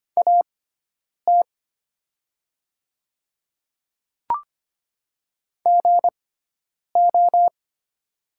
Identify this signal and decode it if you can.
Morse code, heard as a keyed tone, about 25 wpm, spaced as for 9 wpm, 700 Hz. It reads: AT GO
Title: 6x Character Spacing - Top 300 Words - 25wpm